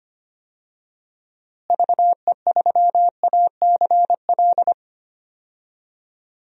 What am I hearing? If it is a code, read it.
Morse code, heard as a keyed tone, about 25 wpm, 700 Hz. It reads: VE3ACL
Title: DXCC-100 Call Signs - 6 Characters - Repeated 3x - 25wpm